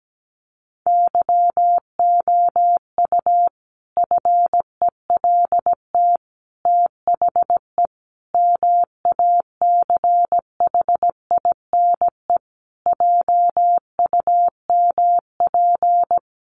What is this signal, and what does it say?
Morse code, heard as a keyed tone, about 17 wpm, 700 Hz. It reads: YOU FELT THE MACHINE JUMP
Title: A Sound Of Thunder 17wpm